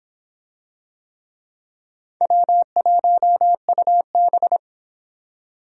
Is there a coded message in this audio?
Yes: Morse code from 2.2 to 4.6 s, heard as a keyed tone, about 25 wpm, 700 Hz. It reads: W1UB